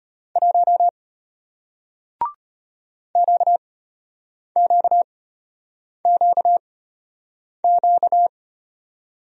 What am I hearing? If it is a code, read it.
Morse code, heard as a keyed tone, about 33 wpm, spaced as for 8 wpm, 700 Hz. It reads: 1 QQQQ